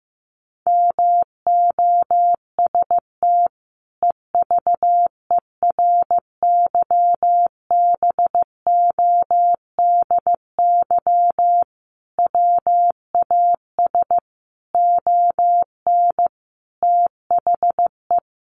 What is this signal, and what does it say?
Morse code, heard as a keyed tone, about 15 wpm, 700 Hz. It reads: MOST EVERYBODY WAS ON THE
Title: Sentences from Top 1400 Words 15wpm